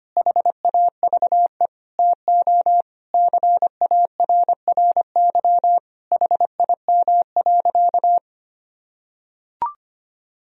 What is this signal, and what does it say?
Morse code, heard as a keyed tone, about 25 wpm, 700 Hz. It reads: HAVE TO CARRY HIM.
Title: Sentences from Top 1000 Words - Review Part 1 - 25wpm